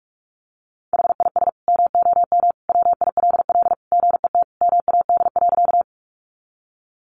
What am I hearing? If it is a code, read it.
Morse code, heard as a keyed tone, about 44 wpm, 700 Hz. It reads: HIS MOM WILL GET MAD.
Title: Additional Easy Phrases 45wpm